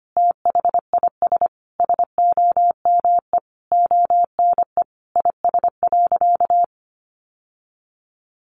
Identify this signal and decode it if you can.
Morse code, heard as a keyed tone, about 25 wpm, 700 Hz. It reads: THIS SOME ONE IS.
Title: Sentences from Top 1100 Words 25wpm